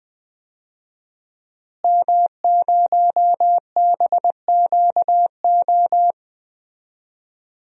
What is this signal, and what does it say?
Morse code, heard as a keyed tone, about 20 wpm, 700 Hz. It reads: M0BQO
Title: DXCC-100 Call Signs - 5 Characters - Repeated 2x - 20wpm